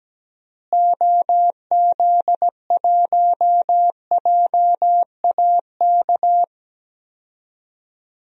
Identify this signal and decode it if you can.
Morse code, heard as a keyed tone, about 17 wpm, 700 Hz. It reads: OZ1JAK
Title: DXCC-100 Call Signs - 6 Characters - Repeated 2x - 17wpm